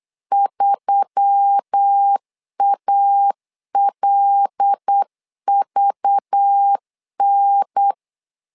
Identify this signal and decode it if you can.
Morse code, heard as a keyed tone, about 8 wpm, 800 Hz. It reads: 3ALVN
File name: LEZIONI / 5 LBVC / 5 LBVC 8wpm 2935